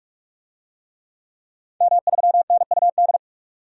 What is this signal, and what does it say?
Morse code, heard as a keyed tone, about 42 wpm, 700 Hz. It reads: M3NUD
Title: DXCC-100 Call Signs - 5 Characters - 45wpm